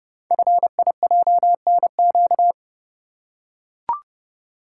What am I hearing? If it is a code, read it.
Morse code, heard as a keyed tone, about 30 wpm, 700 Hz. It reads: FIJNQ
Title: Code Groups - 5 Letters - 30wpm